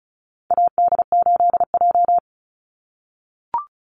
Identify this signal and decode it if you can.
Morse code, heard as a keyed tone, about 34 wpm, 700 Hz. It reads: AD8J
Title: Call Signs - 35wpm